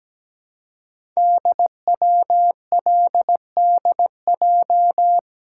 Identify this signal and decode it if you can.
Morse code, heard as a keyed tone, about 17 wpm, 700 Hz. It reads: DWLDJ
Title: Code Groups - 5 Letters - 17wpm